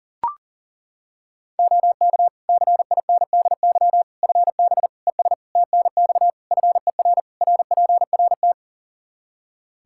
Morse code, heard as a keyed tone, about 40 wpm, 700 Hz. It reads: OK CINDY FB ES TNX FER RPRT
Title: K1USN SST QSO Practice 40wpm